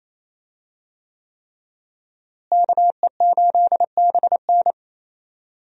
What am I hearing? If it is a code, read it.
Morse code, heard as a keyed tone, about 27 wpm, 700 Hz. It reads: KE8BN
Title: USA Call Signs - 5 Characters - Repeated 3x - 28wpm